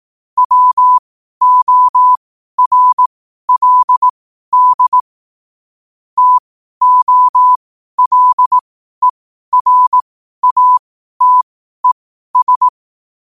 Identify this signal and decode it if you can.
Morse code, heard as a keyed tone, about 18 wpm, spaced as for 12 wpm, 1000 Hz. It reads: WORLD TOLERATES
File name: qotdfiles / 12wpm